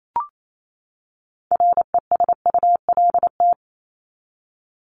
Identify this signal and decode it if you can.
Morse code, heard as a keyed tone, about 28 wpm, 700 Hz. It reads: RESULT
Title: ES:S Suffix - Top 100 Words with Base Words 28wpm